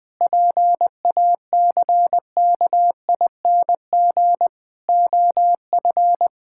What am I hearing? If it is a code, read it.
Morse code, heard as a keyed tone, about 20 wpm, 700 Hz. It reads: PACKING OF
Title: Pangram Challenge v2 - 20wpm